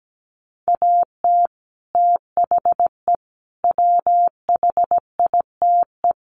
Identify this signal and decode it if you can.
Morse code, heard as a keyed tone, about 17 wpm, 700 Hz. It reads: AT THE WHITE
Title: Sets of 4 Words 17wpm